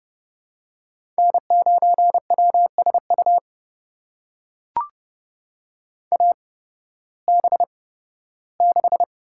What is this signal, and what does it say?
Morse code, heard as a keyed tone, about 32 wpm, spaced as for 16 wpm, 700 Hz. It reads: N9WSU AB6
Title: USA Call Signs - 5 Characters - 8x Character Spacing - 30wpm